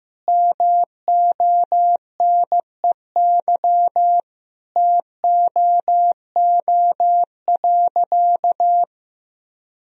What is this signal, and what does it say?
Morse code, heard as a keyed tone, about 15 wpm, 700 Hz. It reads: MONEY TOO.
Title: Sentences from Top 1000 Words - Review Part 4 - 15wpm